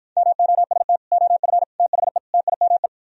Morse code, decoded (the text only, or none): MOST OF THE TIME